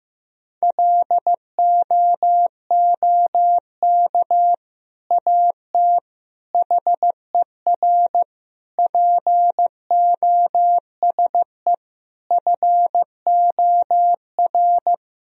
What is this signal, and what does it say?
Morse code, heard as a keyed tone, about 15 wpm, 700 Hz. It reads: LOOK AT HER POSE FOR
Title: Sentences from Top 1000 Words - Review Part 2 - 15wpm